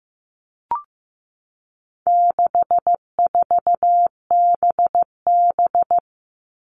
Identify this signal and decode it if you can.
Morse code, heard as a keyed tone, about 15 wpm, 700 Hz. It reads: 64BB
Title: All Cut Numbers - 4-Digit 15wpm